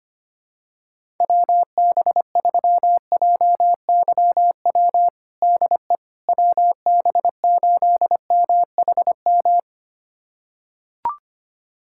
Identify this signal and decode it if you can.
Morse code, heard as a keyed tone, about 25 wpm, 700 Hz. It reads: WB3JYW DE WB8MHM
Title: USA Callsign DE Callsign - 6 Characters - 25wpm